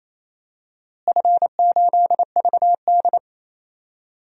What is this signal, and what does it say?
Morse code, heard as a keyed tone, about 28 wpm, 700 Hz. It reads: F8VD
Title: DXCC-100 Call Signs - 4 Characters - Repeated 3x - 28wpm